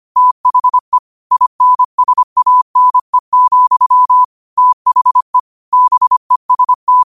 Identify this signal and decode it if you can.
Morse code, heard as a keyed tone, about 25 wpm, 1000 Hz. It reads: THE INSANE, THE BEST
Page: New quotes every day in morse code at 25 Words per minute.